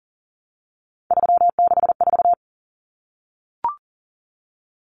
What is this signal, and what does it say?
Morse code, heard as a keyed tone, about 40 wpm, 700 Hz. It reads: 364
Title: Common Cut Numbers - 3-Digit 40wpm